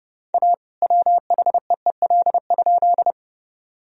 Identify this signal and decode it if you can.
Morse code, heard as a keyed tone, about 30 wpm, 700 Hz. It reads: A WHEEL?
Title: Sentences from Top 1000 Words - Review Part 3 - 30wpm